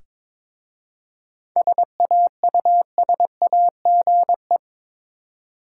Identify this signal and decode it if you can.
Morse code, heard as a keyed tone, about 22 wpm, 700 Hz. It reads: SAUSAGE
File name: Difficult Words with Hints - 22wpm